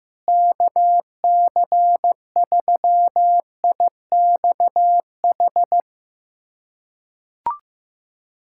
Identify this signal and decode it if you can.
Morse code, heard as a keyed tone, about 15 wpm, 700 Hz. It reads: KC3IXH
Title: USA Callsign DE Callsign 15wpm